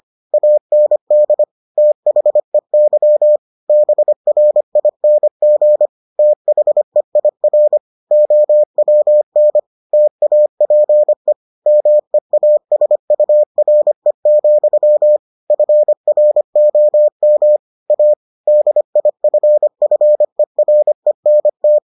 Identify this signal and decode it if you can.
Morse code, heard as a keyed tone, about 25 wpm, 600 Hz. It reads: AND THEY BRING THEIR OWN TAPE MEASURE, FROM A DIFFERENT